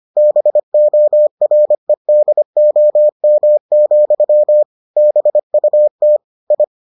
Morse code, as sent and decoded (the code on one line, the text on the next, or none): -... --- .-. -.. --- -- --..-- / -... ..- - / ..
BOREDOM, BUT I